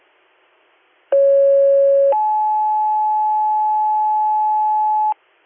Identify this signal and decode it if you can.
Two-tone page, heard as a one-second tone, then a three-second tone, about 550 then 860 Hz, with noise. It is heard from 1.1 to 5.1 s.